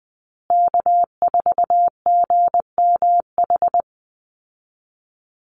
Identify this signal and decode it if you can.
Morse code, heard as a keyed tone, about 20 wpm, 700 Hz. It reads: K4GMH
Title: Call Signs Repeated 2x - 20wpm